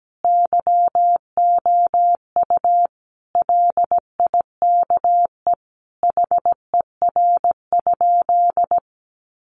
Morse code, heard as a keyed tone, about 17 wpm, 700 Hz. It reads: YOU LIKE HER?
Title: Sentences from Top 100 Words 17wpm